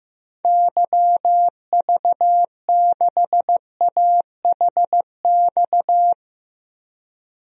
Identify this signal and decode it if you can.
Morse code, heard as a keyed tone, about 15 wpm, 700 Hz. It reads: YV6AHX
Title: DXCC-100 Call Signs - 6 Characters - Repeated 3x - 15wpm